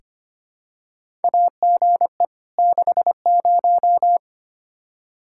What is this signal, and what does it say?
Morse code, heard as a keyed tone, about 25 wpm, 700 Hz. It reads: AGE 60
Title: Practice QSO - Exchange 3 - More Info - 25wpm